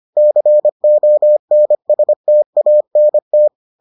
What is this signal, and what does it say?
Morse code, heard as a keyed tone, about 25 wpm, 600 Hz. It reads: CONSTANT